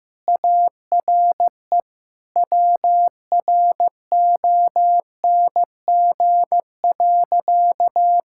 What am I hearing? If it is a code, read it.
Morse code, heard as a keyed tone, about 15 wpm, 700 Hz. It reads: ARE WRONG.
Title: Sentences from Top 1000 Words - Review Part 5 - 15wpm